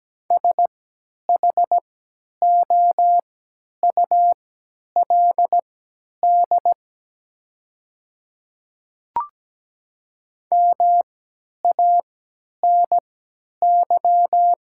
Morse code, heard as a keyed tone, about 17 wpm, spaced as for 10 wpm, 700 Hz. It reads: SHOULD MANY
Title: 3x Character Spacing - Top 300 Words - 17wpm